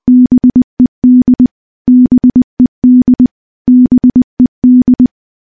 Morse code, heard as a keyed tone, about 20 wpm, 250 Hz. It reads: BED BED BED
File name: morse-training-20WPM